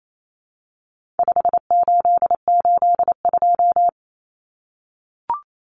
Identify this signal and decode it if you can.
Morse code, heard as a keyed tone, about 28 wpm, 700 Hz. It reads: 5882
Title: Numbers - 4-Digit 28wpm